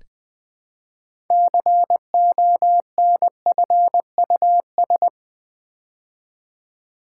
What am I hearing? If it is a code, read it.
Morse code, heard as a keyed tone, about 20 wpm, 700 Hz. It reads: CONFUS
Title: Build A Word - 3 Letters At A Time - 9-Letter Words - 20wpm